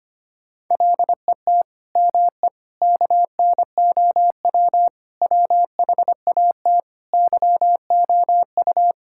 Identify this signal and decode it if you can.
Morse code, heard as a keyed tone, about 25 wpm, 700 Hz. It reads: LET ME KNOW WHAT YOU
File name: Sets of 9 Words - 25wpm